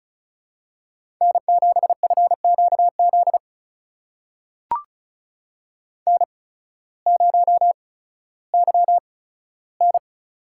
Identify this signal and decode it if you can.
Morse code, heard as a keyed tone, about 36 wpm, spaced as for 16 wpm, 700 Hz. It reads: N7FQZ N0YN
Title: USA Call Signs - 8x Character Spacing - 35wpm